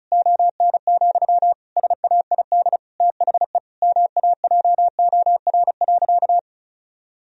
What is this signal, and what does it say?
Morse code, heard as a keyed tone, about 36 wpm, 700 Hz. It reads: ON, SAID THE MAJOR.
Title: Sentences from Top 1000 Words - Review Part 4 - 35wpm